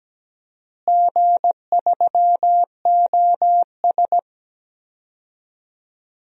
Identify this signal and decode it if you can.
Morse code, heard as a keyed tone, about 17 wpm, 700 Hz. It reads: G3OS
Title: DXCC-100 Call Signs - 4 Characters - Repeated 2x - 17wpm